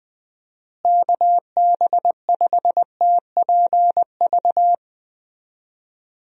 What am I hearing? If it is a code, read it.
Morse code, heard as a keyed tone, about 20 wpm, 700 Hz. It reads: KB5TPV
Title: USA Call Signs - 6 Characters - Repeated 3x - 20wpm